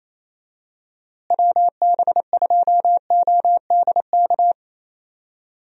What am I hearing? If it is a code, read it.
Morse code, heard as a keyed tone, about 28 wpm, 700 Hz. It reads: WB2ODK
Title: USA Call Signs - 6 Characters - Repeated 2x - 28wpm